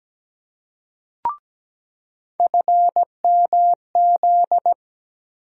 Morse code, heard as a keyed tone, about 17 wpm, 700 Hz. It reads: FMZ
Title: USA Call Sign Postfixes - 3 Letters 17wpm